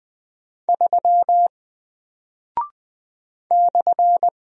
Morse code, heard as a keyed tone, about 20 wpm, 700 Hz. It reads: 3 /
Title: Single Character (Letter:Number:Punctuation:Prosign) Alt 20wpm